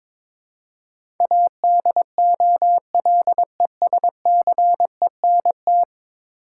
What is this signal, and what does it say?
Morse code, heard as a keyed tone, about 22 wpm, 700 Hz. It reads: ADOLESCENT
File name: practice / 22wpm / 10 Letter Words - First 3 Letters Given - 22wpm